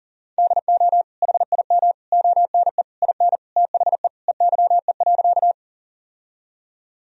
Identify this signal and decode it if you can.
Morse code, heard as a keyed tone, about 37 wpm, 700 Hz. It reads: DO HIM ONE IN THE EYE.
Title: Sentences from Top 1000 Words - Review Part 1 - 40wpm